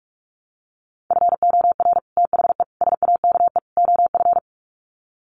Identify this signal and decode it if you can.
Morse code, heard as a keyed tone, about 44 wpm, 700 Hz. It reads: FOR THE SAKE OF
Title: Sets of 4 Words 45wpm